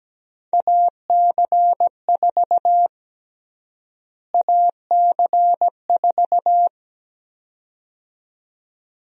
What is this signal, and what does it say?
Morse code, heard as a keyed tone, about 17 wpm, 700 Hz. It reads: AC4 AC4
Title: USA Call Sign Prefixes - Repeated 2x - 17wpm